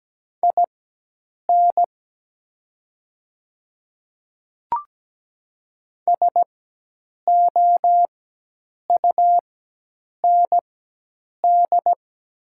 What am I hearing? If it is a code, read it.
Morse code, heard as a keyed tone, about 17 wpm, spaced as for 8 wpm, 700 Hz. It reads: IN SOUND
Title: 4x Character Spacing - Top 300 Words - 17wpm